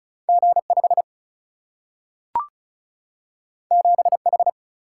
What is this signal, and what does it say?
Morse code, heard as a keyed tone, about 36 wpm, 700 Hz. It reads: G5 7H
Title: Single-Letter Single-Number 35wpm